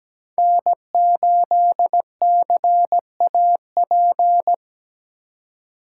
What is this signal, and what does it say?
Morse code, heard as a keyed tone, about 17 wpm, 700 Hz. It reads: N8CAP
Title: USA Call Signs - 5 Characters - 2x Character Spacing - 17wpm